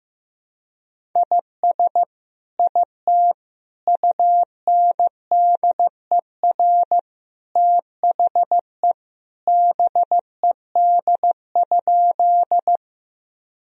Morse code, heard as a keyed tone, about 15 wpm, 700 Hz. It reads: IS IT UNDER THE BED?
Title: Sentences from Top 1000 Words - Review Part 1 - 15wpm